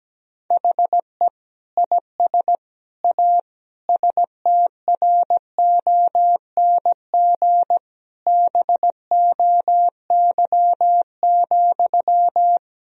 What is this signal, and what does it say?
Morse code, heard as a keyed tone, about 17 wpm, 700 Hz. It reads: HE IS A STRONG BOY,
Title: Sentences from Top 1000 Words - Review Part 3 - 17wpm